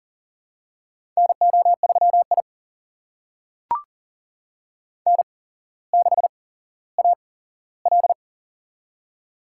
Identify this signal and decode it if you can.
Morse code, heard as a keyed tone, about 41 wpm, spaced as for 17 wpm, 700 Hz. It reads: NO3I N6AL
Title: USA Call Signs - 4 Characters - 8x Character Spacing - 40wpm